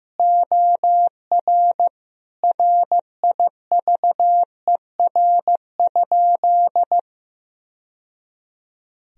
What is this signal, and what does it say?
Morse code, heard as a keyed tone, about 15 wpm, 700 Hz. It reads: OR RIVER?
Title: Sentences from Top 1000 Words - Review Part 1 - 15wpm